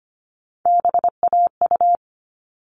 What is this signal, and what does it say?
Morse code, heard as a keyed tone, about 25 wpm, 700 Hz. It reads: BAU
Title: All Cut Numbers - 3-Digit 25wpm